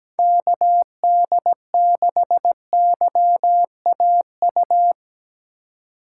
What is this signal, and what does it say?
Morse code, heard as a keyed tone, about 17 wpm, 700 Hz. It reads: KD6YAU
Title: USA Call Signs - 2x Character Spacing - 17wpm